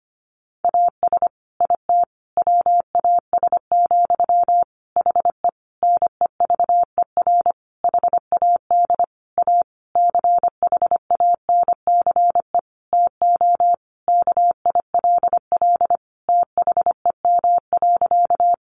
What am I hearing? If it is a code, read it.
Morse code, heard as a keyed tone, about 25 wpm, 700 Hz. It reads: AS IT WAS, HE NEVER HAD A CHANCE TO KILL THEM.
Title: Sentences from Top 1400 Words 25wpm